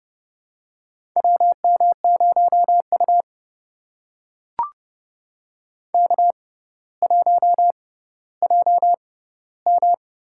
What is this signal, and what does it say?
Morse code, heard as a keyed tone, about 30 wpm, spaced as for 15 wpm, 700 Hz. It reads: WM0U K1JM
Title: USA Call Signs - 4 Characters - 6x Character Spacing - 30wpm